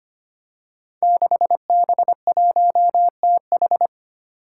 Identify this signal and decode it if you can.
Morse code, heard as a keyed tone, about 25 wpm, 700 Hz. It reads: 6B1TH
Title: Five Challenging Letter-Number Combinations 25wpm